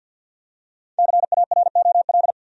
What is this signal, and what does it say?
Morse code, heard as a keyed tone, about 49 wpm, 700 Hz. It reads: CAROL
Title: USA Common Names - Max 5 Letters - 50wpm